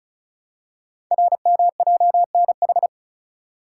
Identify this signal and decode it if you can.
Morse code, heard as a keyed tone, about 36 wpm, 690 Hz. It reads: RMJNH
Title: Code Groups - 5 Letters - 35wpm